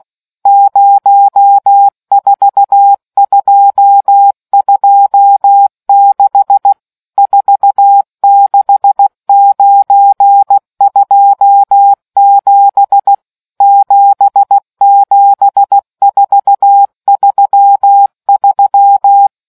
Morse code, heard as a keyed tone, about 16 wpm, 800 Hz. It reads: 04226 46927 77433